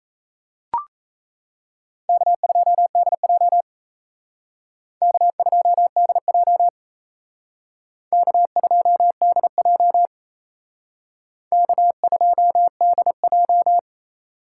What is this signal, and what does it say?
Morse code, heard as a keyed tone, about 35 wpm, 700 Hz. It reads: K2DJ K2DJ K2DJ K2DJ
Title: Speed Racing - USA 1x2 Call Signs 28wpm